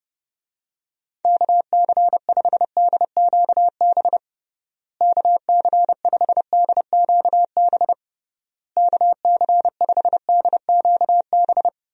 Morse code, heard as a keyed tone, about 32 wpm, 700 Hz. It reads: KC5DQB KC5DQB KC5DQB
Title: USA Call Signs - 6 Characters - Repeated 3x - 30wpm